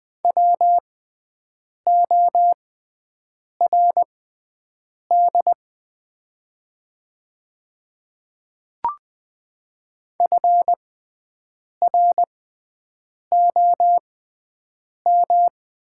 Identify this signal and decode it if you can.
Morse code, heard as a keyed tone, about 20 wpm, spaced as for 7 wpm, 700 Hz. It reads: WORD FROM
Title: 6x Character Spacing - Top 100 Words - 20wpm